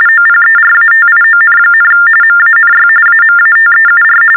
TTY/TDD text telephone tones, about 45.45 baud. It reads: OPERATION, MANY AMATEUR
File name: teletype-RTTY-45